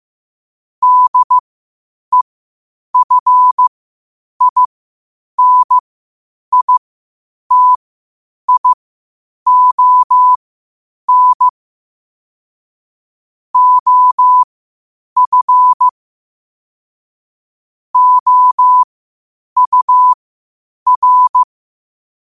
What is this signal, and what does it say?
Morse code, heard as a keyed tone, about 15 wpm, spaced as for 8 wpm, 1000 Hz. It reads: DEFINITION OF OUR
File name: qotdfiles / 8wpm